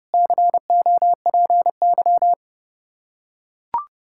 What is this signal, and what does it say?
Morse code, heard as a keyed tone, about 30 wpm, 700 Hz. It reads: COPY
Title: Practice QSO - Exchange 3 - More Info - 30wpm